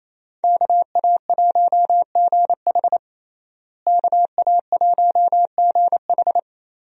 Morse code, heard as a keyed tone, about 28 wpm, 700 Hz. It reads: KA1GH KA1GH
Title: USA Call Signs - 5 Characters - Repeated 3x - 28wpm